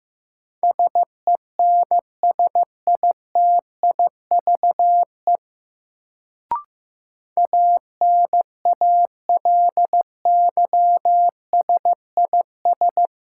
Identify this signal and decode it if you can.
Morse code, heard as a keyed tone, about 15 wpm, 700 Hz. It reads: SENSITIVE ANALYSIS
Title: Sentences from Top 1400 Words - New Word Drill Alt - 15wpm